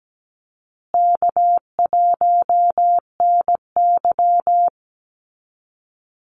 Morse code, heard as a keyed tone, about 17 wpm, 700 Hz. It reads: K1NY
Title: Easy Call Signs Repeated 3x - 17wpm